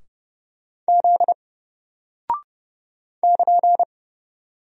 Morse code, heard as a keyed tone, about 30 wpm, 700 Hz. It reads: Z <KN>
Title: Single Character (Letter:Number:Punctuation:Prosign) Alt 30wpm